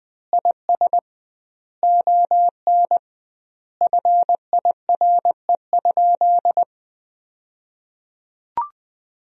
Morse code, heard as a keyed tone, about 20 wpm, 700 Hz. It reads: IS ON FIRE?
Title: 2x Word-Spacing - Sentences from Top 400 Words - 20wpm